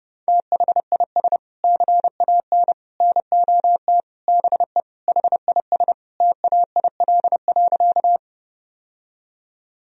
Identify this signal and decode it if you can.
Morse code, heard as a keyed tone, about 30 wpm, 700 Hz. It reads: THIS CAN NOT BE HIS TAIL.
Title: Sentences from Top 1000 Words - Review Part 3 - 30wpm